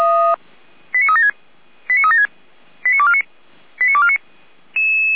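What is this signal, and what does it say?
Five-tone selective call (ZVEI), heard as quick runs of five tones, tones of about 70 ms. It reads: 89376 89376 89249 89249
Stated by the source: ZVEI five-tone selective call received on SDR, Feuerwehr Orlamünde.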